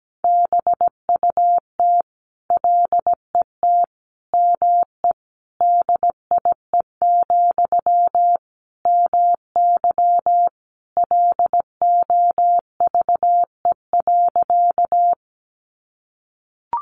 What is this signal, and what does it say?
Morse code, heard as a keyed tone, about 17 wpm, 700 Hz. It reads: BUT LET ME DIE, MY LOVE.
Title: Sentences from Top 1300 Words 17wpm